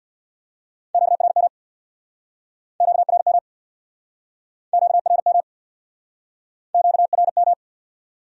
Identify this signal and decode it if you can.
Morse code, heard as a keyed tone, about 66 wpm, 700 Hz. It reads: QRM QRM QRM QRM